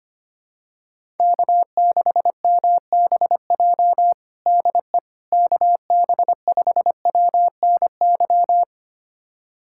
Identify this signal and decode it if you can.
Morse code, heard as a keyed tone, about 25 wpm, 700 Hz. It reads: K6MBJ DE KB5WNY
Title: USA Callsign DE Callsign 25wpm